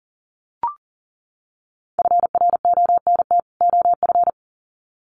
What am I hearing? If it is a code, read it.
Morse code, heard as a keyed tone, about 39 wpm, 700 Hz. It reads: FRONT OF
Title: Sets of 2 Words 40wpm